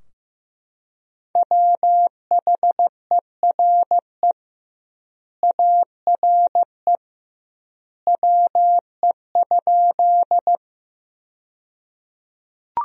Morse code, heard as a keyed tone, about 15 wpm, 700 Hz. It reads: WHERE ARE WE?
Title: 2x Word-Spacing - Sentences from Top 200 Words - 15wpm